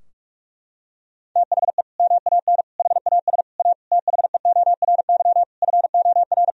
Morse code, heard as a keyed tone, about 46 wpm, 700 Hz. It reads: THE MAN HAS A THEORY FOR
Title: Sentences from Top 1000 Words - Review Part 3 - 45wpm